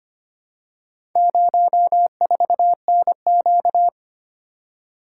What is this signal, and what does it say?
Morse code, heard as a keyed tone, about 25 wpm, 700 Hz. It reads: 04NQ
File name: practice / 25wpm / Four Letter-Number Combinations - 25wpm